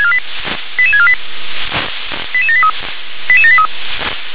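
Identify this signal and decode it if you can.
Five-tone selective call (ZVEI), heard as quick runs of five tones, tones of about 70 ms. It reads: 88648 88783 88783
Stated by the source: ZVEI five-tone selective call 88913 received via SDR.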